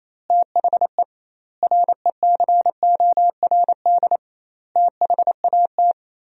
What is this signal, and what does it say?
Morse code, heard as a keyed tone, about 27 wpm, 700 Hz. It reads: THE RECORD THAT
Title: 2x Word-Spacing - Sentences from Top 500 Words - 28wpm